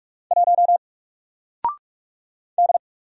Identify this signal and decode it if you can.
Morse code, heard as a keyed tone, about 44 wpm, 700 Hz. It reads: 1 D
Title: Single Character (Letter:Number:Punctuation:Prosign) 45wpm